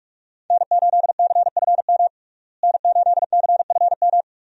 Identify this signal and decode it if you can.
Morse code, heard as a keyed tone, about 47 wpm, 700 Hz. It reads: N8CFM N8CFM
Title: USA Call Signs - 5 Characters - Repeated 3x - 45wpm